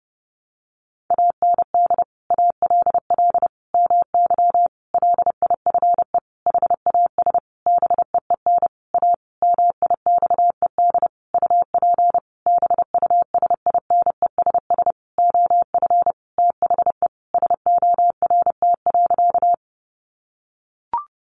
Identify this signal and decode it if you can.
Morse code, heard as a keyed tone, about 30 wpm, 700 Hz. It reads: AND ALL MY LIFE HAS BEEN A MIXED UP BUSINESS OF THE SORT.
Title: Sentences from Top 1500 Words 30wpm